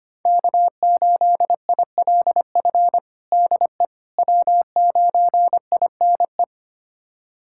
Morse code, heard as a keyed tone, about 25 wpm, 700 Hz. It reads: K8ILF DE W9INE
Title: USA Callsign DE Callsign - 5 Characters - 25wpm